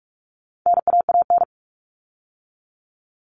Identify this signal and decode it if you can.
Morse code, heard as a keyed tone, about 44 wpm, 700 Hz. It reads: NAAN
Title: All Cut Numbers - 4-Digit 45wpm